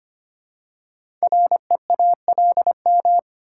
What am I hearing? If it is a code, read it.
Morse code, heard as a keyed tone, about 25 wpm, 700 Hz. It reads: REALM
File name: 5 Letter Words - First 3 Letters Given - 25wpm